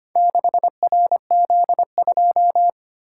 Morse code, heard as a keyed tone, about 25 wpm, 700 Hz. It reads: 6RZ2
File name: practice / 25wpm / Four Letter-Number Combinations - 25wpm